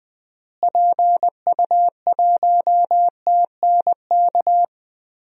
Morse code, heard as a keyed tone, about 20 wpm, 700 Hz. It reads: PU1TNK
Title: DXCC-100 Call Signs - 6 Characters - Repeated 3x - 20wpm